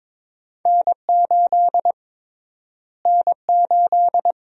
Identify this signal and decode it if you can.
Morse code, heard as a keyed tone, about 22 wpm, 700 Hz. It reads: N8 N8
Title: USA Call Sign Prefixes - Repeated 2x - 22wpm